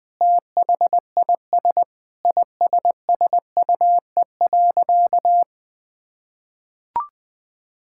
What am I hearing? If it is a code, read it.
Morse code, heard as a keyed tone, about 20 wpm, 700 Hz. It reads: THIS ISSUE.
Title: Sentences with 7-Word Phrases - Spoken Beginning - 20wpm